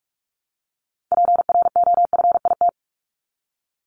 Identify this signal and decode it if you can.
Morse code, heard as a keyed tone, about 44 wpm, 700 Hz. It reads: PROFIT
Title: Top 751-1000 Words 45wpm